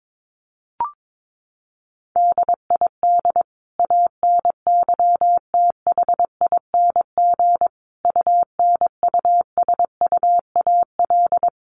Morse code, heard as a keyed tone, about 22 wpm, 700 Hz. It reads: DID ANYTHING UNUSUAL
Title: Sentences from Top 1400 Words 22wpm